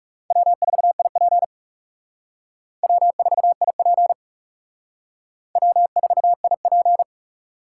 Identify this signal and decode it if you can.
Morse code, heard as a keyed tone, about 37 wpm, 700 Hz. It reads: W4IP W4IP W4IP
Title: Speed Racing - USA 1x2 Call Signs 30wpm